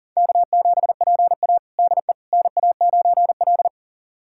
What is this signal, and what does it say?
Morse code, heard as a keyed tone, about 40 wpm, 700 Hz. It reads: K7PA DE NA9L
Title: USA Callsign DE Callsign - 4 Characters - 40wpm